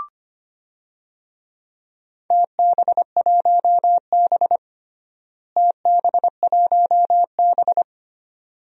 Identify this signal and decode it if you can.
Morse code, heard as a keyed tone, about 25 wpm, 700 Hz. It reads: TB1B TB1B
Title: DXCC-100 Call Signs - 4 Characters - Repeated 2x - 25wpm